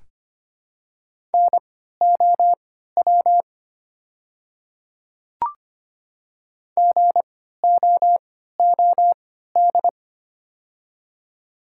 Morse code, heard as a keyed tone, about 25 wpm, spaced as for 14 wpm, 700 Hz. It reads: NOW GOOD
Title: 3x Character Spacing - Top 300 Words - 25wpm